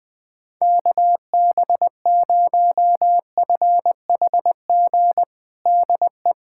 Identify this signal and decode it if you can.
Morse code, heard as a keyed tone, about 21 wpm, 700 Hz. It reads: KB0FHG DE
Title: USA Callsign DE Callsign - 6 Characters - 20wpm